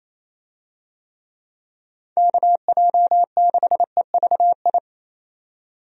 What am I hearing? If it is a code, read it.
Morse code, heard as a keyed tone, about 28 wpm, 700 Hz. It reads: KJ6EVI
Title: USA Call Signs - 6 Characters - 28wpm